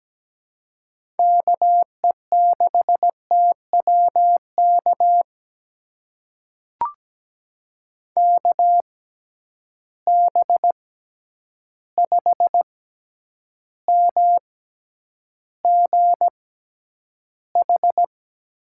Morse code, heard as a keyed tone, about 17 wpm, spaced as for 9 wpm, 700 Hz. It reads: KE6TWK KB5MGH
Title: USA Call Signs - 6 Characters - 6x Character Spacing - 17wpm